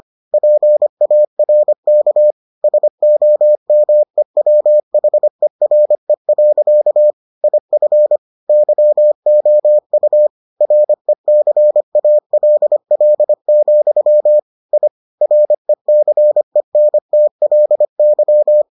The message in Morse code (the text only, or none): PARK SOMEWHERE. IF YOU RECALL, I RECENTLY